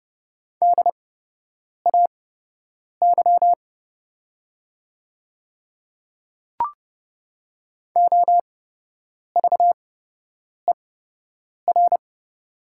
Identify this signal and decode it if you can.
Morse code, heard as a keyed tone, about 30 wpm, spaced as for 8 wpm, 700 Hz. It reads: DAY OVER
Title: 8x Character Spacing - Top 100 Words - 30wpm